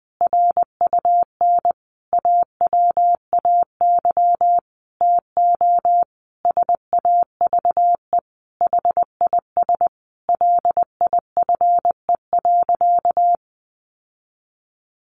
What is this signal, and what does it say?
Morse code, heard as a keyed tone, about 20 wpm, 700 Hz. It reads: RUN AWAY TO SAVE HIS LIFE.
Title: Sentences from Top 1100 Words 20wpm